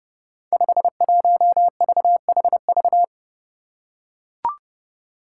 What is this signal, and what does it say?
Morse code, heard as a keyed tone, about 32 wpm, 700 Hz. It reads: 51VHV
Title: Five Challenging Letter-Number Combinations 30wpm